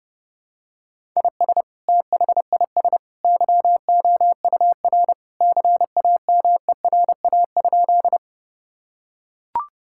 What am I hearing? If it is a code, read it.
Morse code, heard as a keyed tone, about 30 wpm, 700 Hz. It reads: IS THIS YOUR CAMERA?
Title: Sentences from Top 1000 Words - Review Part 5 - 30wpm